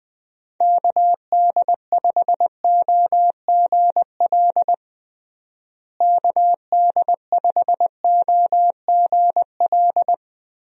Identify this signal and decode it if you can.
Morse code, heard as a keyed tone, about 19 wpm, 700 Hz. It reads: KD5OGL KD5OGL
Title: USA Call Signs - 6 Characters - Repeated 3x - 20wpm